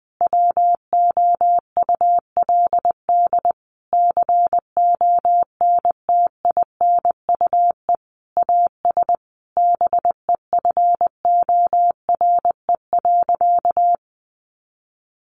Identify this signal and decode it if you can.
Morse code, heard as a keyed tone, about 19 wpm, 700 Hz. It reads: WOULD CONTINUE AS BEFORE.
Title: Sentences from Top 1400 Words 20wpm